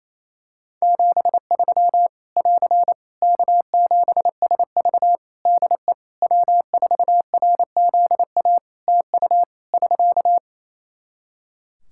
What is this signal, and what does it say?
Morse code, heard as a keyed tone, about 28 wpm, 700 Hz. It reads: 73 <AR> K7SV DE W4RZA TU <SK>
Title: Practice QSO - Exchange 4 - Ending - 28wpm